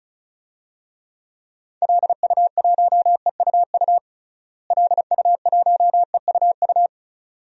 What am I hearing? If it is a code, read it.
Morse code, heard as a keyed tone, about 34 wpm, 700 Hz. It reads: LU1EUU LU1EUU
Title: DXCC-100 Call Signs - 6 Characters - Repeated 2x - 35wpm